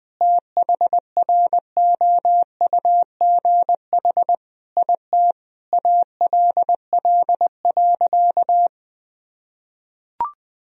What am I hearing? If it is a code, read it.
Morse code, heard as a keyed tone, about 20 wpm, 700 Hz. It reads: THROUGH IT ALL.
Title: Sentences from Top 1000 Words - Review Part 1 - 20wpm